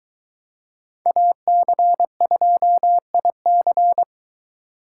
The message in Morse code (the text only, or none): AC2IC